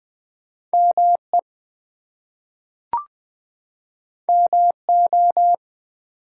Morse code, heard as a keyed tone, about 20 wpm, 700 Hz. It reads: ME MO
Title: US States and Canadian Provinces 20wpm